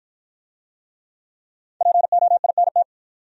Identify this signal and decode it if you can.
Morse code, heard as a keyed tone, about 51 wpm, 700 Hz. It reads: POINT